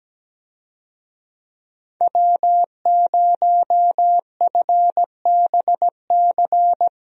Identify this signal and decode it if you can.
Morse code, heard as a keyed tone, about 17 wpm, 700 Hz. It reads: W0FBC
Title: USA Call Signs - 5 Characters - 17wpm